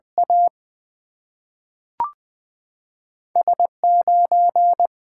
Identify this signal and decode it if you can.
Morse code, heard as a keyed tone, about 20 wpm, 700 Hz. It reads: A S 9
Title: Two Letter-Number Combinations - 20wpm